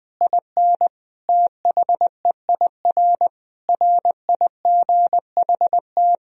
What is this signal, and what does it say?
Morse code, heard as a keyed tone, about 20 wpm, 700 Hz. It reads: IN THEIR RIGHT
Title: Sentences from Top 1000 Words - Review Part 1 - 20wpm